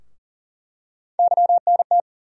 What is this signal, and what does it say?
Morse code, heard as a keyed tone, about 40 wpm, 700 Hz. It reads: YNT
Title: USA Call Sign Postfixes - 3 Letters 40wpm